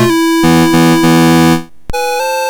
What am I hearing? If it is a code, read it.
There is no code.